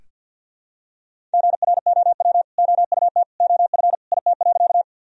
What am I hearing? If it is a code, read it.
Morse code, heard as a keyed tone, about 46 wpm, 700 Hz. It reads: GROW OUT OF IT.
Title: Sentences from Top 1000 Words - Review Part 1 - 50wpm